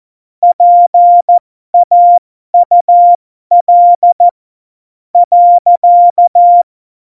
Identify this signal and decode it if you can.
Morse code, heard as a keyed tone, about 14 wpm, spaced as for 12 wpm, 700 Hz. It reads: PAUL .
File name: Morsekurs / Witze